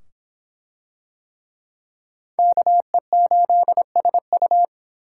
Morse code, heard as a keyed tone, about 22 wpm, 700 Hz. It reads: KE8SU